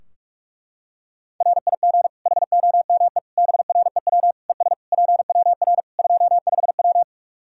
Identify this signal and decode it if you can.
Morse code, heard as a keyed tone, about 46 wpm, 700 Hz. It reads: RIG HOME BREW ES PWR 25W
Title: Practice QSO - Exchange 2 - Info - 45wpm